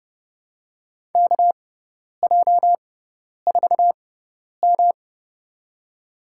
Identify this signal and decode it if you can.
Morse code, heard as a keyed tone, about 30 wpm, spaced as for 10 wpm, 700 Hz. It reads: KJ4M
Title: USA Call Signs - 4 Characters - 6x Character Spacing - 30wpm